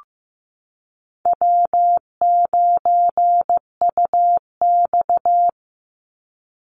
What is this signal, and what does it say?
Morse code, heard as a keyed tone, about 15 wpm, 700 Hz. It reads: W9UX
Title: NA SOTA Chaser Callsigns 15wpm